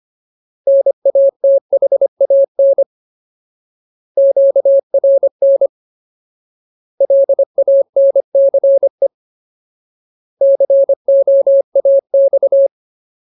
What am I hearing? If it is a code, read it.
Morse code, heard as a keyed tone, about 25 wpm, 550 Hz. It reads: NATHAN QRN LANCE COAX
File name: Ultimate Word List / 25WPM Ultimate